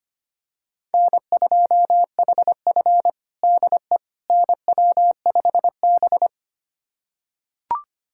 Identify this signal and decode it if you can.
Morse code, heard as a keyed tone, about 25 wpm, 700 Hz. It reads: N2HF DE NW5B
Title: USA Callsign DE Callsign - 4 Characters - 25wpm